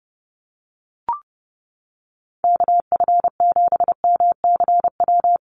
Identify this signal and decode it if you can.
Morse code, heard as a keyed tone, about 30 wpm, 700 Hz. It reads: KF7MCW
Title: USA Call Signs Repeated 2x 30wpm